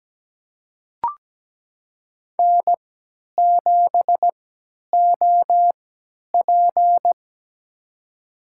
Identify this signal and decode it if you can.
Morse code, heard as a keyed tone, about 17 wpm, spaced as for 10 wpm, 700 Hz. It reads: N7OP
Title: USA Call Signs - 4 Characters - 3x Character Spacing - 17wpm